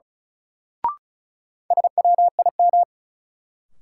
Morse code, heard as a keyed tone, about 36 wpm, 700 Hz. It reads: SWIM
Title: Sentences from Top 1000 Words - Review Part 6 - New Word Drill - 35wpm